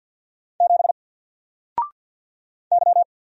Morse code, heard as a keyed tone, about 50 wpm, 700 Hz. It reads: 7 Y
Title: Single Letter-Number 50wpm